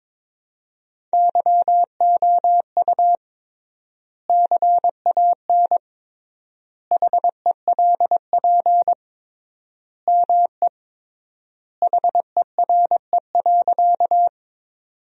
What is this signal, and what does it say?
Morse code, heard as a keyed tone, about 22 wpm, 700 Hz. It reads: YOU CAN HELP ME HERE.
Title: 3x Word-Spacing - Sentences from Top 200 Words - 22wpm